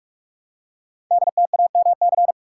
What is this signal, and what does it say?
Morse code, heard as a keyed tone, about 44 wpm, 700 Hz. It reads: DTAMC
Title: Code Groups - 5 Letters - 45wpm